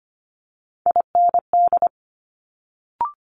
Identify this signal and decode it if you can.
Morse code, heard as a keyed tone, about 25 wpm, 700 Hz. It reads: IND
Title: Multi-letter Phonemes Quick Fire 25wpm